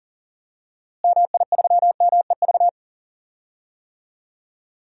Morse code, heard as a keyed tone, about 40 wpm, 700 Hz. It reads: MI3MEV
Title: DXCC-100 Call Signs - 6 Characters - Repeated 2x - 40wpm